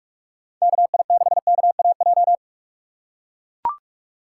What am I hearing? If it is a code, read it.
Morse code, heard as a keyed tone, about 41 wpm, 700 Hz. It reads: KI6KAJ
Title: USA Call Signs - 6 Characters - 3x Character Spacing - 45wpm